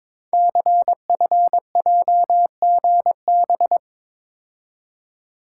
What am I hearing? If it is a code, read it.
Morse code, heard as a keyed tone, about 22 wpm, 700 Hz. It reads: CFJGB
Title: Code Groups - 5 Letters - 22wpm